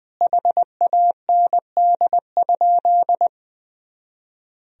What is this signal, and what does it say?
Morse code, heard as a keyed tone, about 20 wpm, 700 Hz. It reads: HAND?
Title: Sentences from Top 200 Words and Abbreviations 20wpm